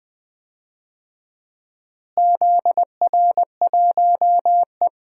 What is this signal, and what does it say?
Morse code, heard as a keyed tone, about 20 wpm, 700 Hz. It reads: ZR1E
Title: DXCC-100 Call Signs - 4 Characters - Repeated 2x - 20wpm